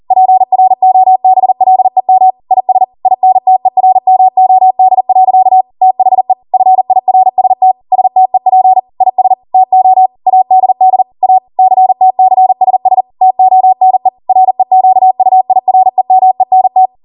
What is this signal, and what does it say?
Morse code, heard as a keyed tone, about 40 wpm, 750 Hz. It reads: PROBLEM IS INTERMOD. THE FIRST STEP IS TO ADD A CTCSS TONE REQUIREMENT
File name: Morse / 40WPM